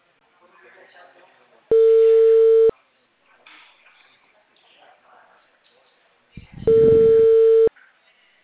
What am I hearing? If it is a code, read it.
Ringback tone (China, 450 Hz), with speech.